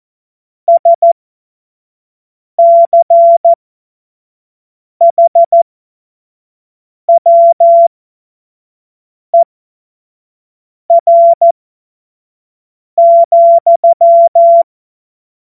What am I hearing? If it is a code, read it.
Morse code, heard as a keyed tone, about 14 wpm, spaced as for 5 wpm, 680 Hz. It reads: SCHWER,
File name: Morsekurs / Witze / Witz 31.Oktober 2023 25 F70